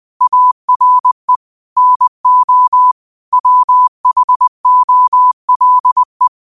Morse code, heard as a keyed tone, about 20 wpm, 1000 Hz. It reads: ARE NO WHOLE
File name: qotdfiles / 20wpm